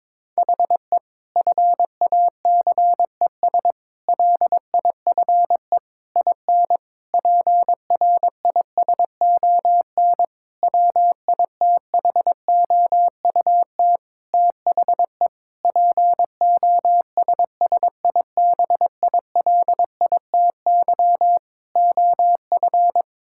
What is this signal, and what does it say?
Morse code, heard as a keyed tone, about 21 wpm, 700 Hz. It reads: HE FACES LIFE IN PRISON WITHOUT THE POSSIBILITY OF
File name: practice / 22wpm / Sentences with 8-Word Phrases - 22wpm